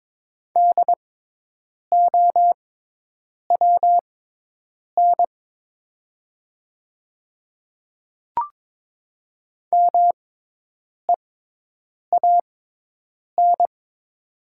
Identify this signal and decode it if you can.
Morse code, heard as a keyed tone, about 22 wpm, spaced as for 8 wpm, 700 Hz. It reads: DOWN MEAN
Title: 6x Character Spacing - Top 200 Words - 22wpm